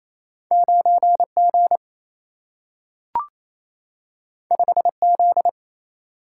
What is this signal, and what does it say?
Morse code, heard as a keyed tone, about 28 wpm, 700 Hz. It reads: 9 G 5 Z
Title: Single-Letter Single-Number 28wpm